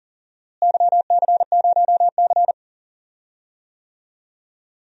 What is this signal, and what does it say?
Morse code, heard as a keyed tone, about 40 wpm, 700 Hz. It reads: YC0C
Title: DXCC-100 Call Signs - 4 Characters - Repeated 2x - 40wpm